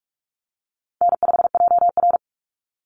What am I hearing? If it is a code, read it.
Morse code, heard as a keyed tone, about 44 wpm, 700 Hz. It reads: N5JR
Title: Easy Call Signs Repeated 2x - 45wpm